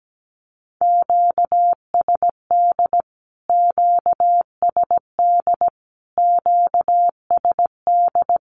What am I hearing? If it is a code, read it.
Morse code, heard as a keyed tone, about 17 wpm, 700 Hz. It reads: QSD QSD QSD
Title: Q-Codes Repeated 3x 17wpm